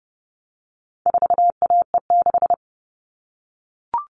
Morse code, heard as a keyed tone, about 29 wpm, 700 Hz. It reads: 4AE6
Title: All Cut Numbers - 4-Digit 30wpm